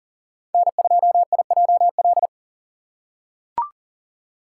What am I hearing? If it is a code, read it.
Morse code, heard as a keyed tone, about 40 wpm, 700 Hz. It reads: N2IJL
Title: USA Call Signs - 6x Character Spacing - 40wpm